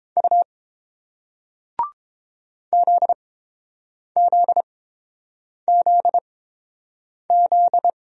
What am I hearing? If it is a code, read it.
Morse code, heard as a keyed tone, about 30 wpm, spaced as for 7 wpm, 700 Hz. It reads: U ZZZZ